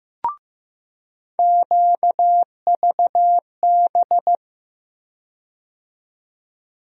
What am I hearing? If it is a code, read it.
Morse code, heard as a keyed tone, about 15 wpm, 700 Hz. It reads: QVB
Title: USA Call Sign Postfixes - 3 Letters 15wpm